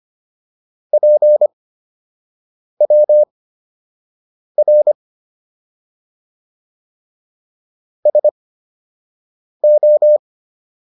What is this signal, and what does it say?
Morse code, heard as a keyed tone, about 25 wpm, spaced as for 6 wpm, 600 Hz. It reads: PWR SO